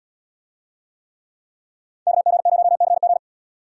Morse code, heard as a keyed tone, about 78 wpm, 700 Hz. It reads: KG0CG